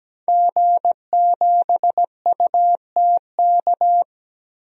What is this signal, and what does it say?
Morse code, heard as a keyed tone, about 17 wpm, 700 Hz. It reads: G7UTK
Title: DXCC-100 Call Signs - 5 Characters - Repeated 2x - 17wpm